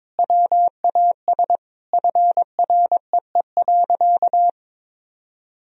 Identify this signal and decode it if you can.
Morse code, heard as a keyed tone, about 22 wpm, 700 Hz. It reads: WAS FREE.
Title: Sentences from Top 1000 Words - Review Part 4 - 22wpm